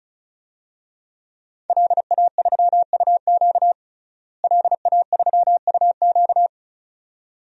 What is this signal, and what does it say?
Morse code, heard as a keyed tone, about 36 wpm, 700 Hz. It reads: LA3UQ LA3UQ
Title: DXCC-100 Call Signs - 5 Characters - Repeated 2x - 35wpm